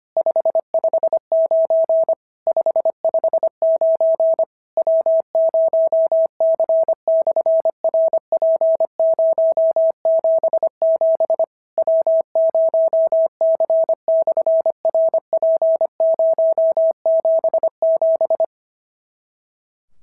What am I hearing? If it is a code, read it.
Morse code, heard as a keyed tone, about 25 wpm, 650 Hz. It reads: 559 559 W0C/RP077 W0C/RP077
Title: SOTA QSO Practice 25wpm